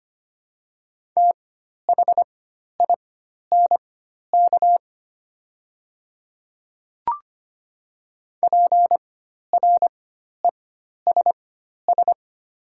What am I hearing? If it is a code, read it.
Morse code, heard as a keyed tone, about 25 wpm, spaced as for 12 wpm, 700 Hz. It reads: THINK PRESS
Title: 4x Character Spacing - Top 500 Words - 25wpm